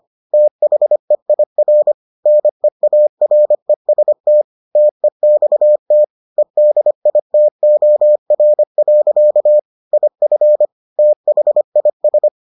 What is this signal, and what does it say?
Morse code, heard as a keyed tone, about 25 wpm, 600 Hz. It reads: THEIR NEAREST TEXT EDITOR. IF THIS